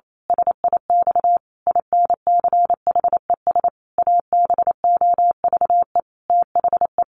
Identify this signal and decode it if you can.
Morse code, heard as a keyed tone, about 28 wpm, 700 Hz. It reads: SIX INCHES ABOVE THE
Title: A Sound Of Thunder 28wpm